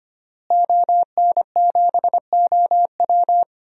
Morse code, heard as a keyed tone, about 25 wpm, 700 Hz. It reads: ON7OW
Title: DXCC-100 Call Signs - 5 Characters - Repeated 3x - 25wpm